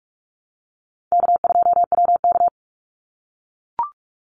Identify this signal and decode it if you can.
Morse code, heard as a keyed tone, about 44 wpm, 700 Hz. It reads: K2WK
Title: Easy Call Signs - 45wpm